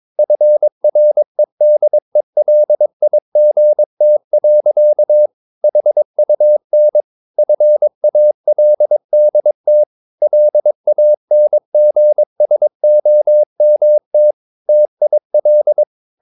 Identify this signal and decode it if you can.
Morse code, heard as a keyed tone, about 22 wpm, 600 Hz. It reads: FREDELIGT. HUN FALDT LANGSOMT TIL
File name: Ferie QRS 22wpm